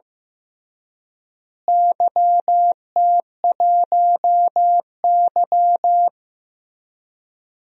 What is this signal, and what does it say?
Morse code, heard as a keyed tone, about 16 wpm, 700 Hz. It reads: YT1Y
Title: DXCC-100 Call Signs - 4 Characters - Repeated 2x - 15wpm